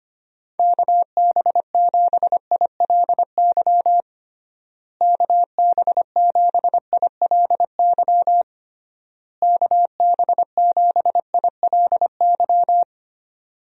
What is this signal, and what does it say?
Morse code, heard as a keyed tone, about 25 wpm, 700 Hz. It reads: KB7ILY KB7ILY KB7ILY
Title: USA Call Signs - 6 Characters - Repeated 3x - 25wpm